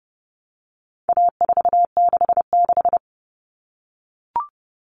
Morse code, heard as a keyed tone, about 30 wpm, 700 Hz. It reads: A466
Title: Common Cut Numbers - 4-Digit 30wpm